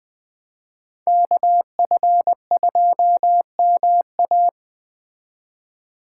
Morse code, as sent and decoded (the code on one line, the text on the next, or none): -.- ..-. ..--- -- .-
KF2MA